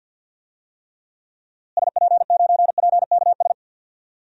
Morse code, heard as a keyed tone, about 47 wpm, 700 Hz. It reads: SP9PKS
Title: DXCC-100 Call Signs - 6 Characters - Repeated 2x - 50wpm